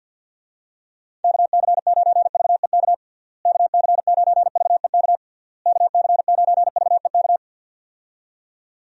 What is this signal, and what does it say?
Morse code, heard as a keyed tone, about 52 wpm, 700 Hz. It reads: KC9VEK KC9VEK KC9VEK
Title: USA Call Signs - 6 Characters - Repeated 3x - 50wpm